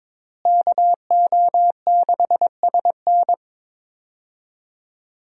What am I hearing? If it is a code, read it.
Morse code, heard as a keyed tone, about 22 wpm, 700 Hz. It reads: KO6SN
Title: USA Call Signs - 5 Characters - Repeated 3x - 22wpm